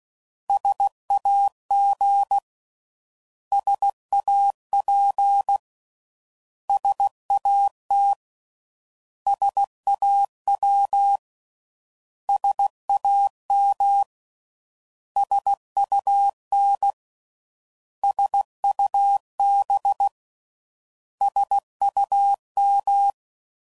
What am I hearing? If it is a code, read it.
Morse code, heard as a keyed tone, about 16 wpm, 800 Hz. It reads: SAG SAP SAT SAW SAM SUN SUB SUM